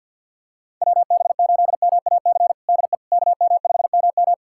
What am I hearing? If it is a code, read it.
Morse code, heard as a keyed tone, about 49 wpm, 700 Hz. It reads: WB8MAC DE KM5MM
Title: USA Callsign DE Callsign 50wpm